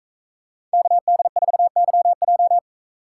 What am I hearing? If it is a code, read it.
Morse code, heard as a keyed tone, about 43 wpm, 700 Hz. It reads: KD4YJ